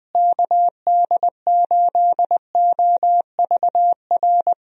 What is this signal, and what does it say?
Morse code, heard as a keyed tone, about 21 wpm, 700 Hz. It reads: KD8OVR
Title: USA Call Signs - 6 Characters - Repeated 2x - 20wpm